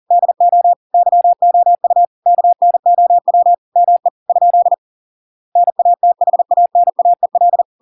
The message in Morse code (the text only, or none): DO YOU KNOW ME ? NATHANAEL